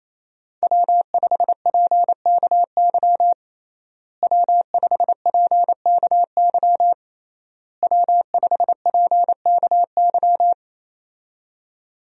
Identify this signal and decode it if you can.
Morse code, heard as a keyed tone, about 28 wpm, 700 Hz. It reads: W5PKY W5PKY W5PKY
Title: USA Call Signs - 5 Characters - Repeated 3x - 28wpm